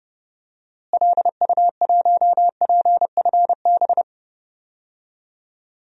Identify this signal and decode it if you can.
Morse code, heard as a keyed tone, about 32 wpm, 700 Hz. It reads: LU1PFB
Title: DXCC-100 Call Signs - 6 Characters - Repeated 2x - 30wpm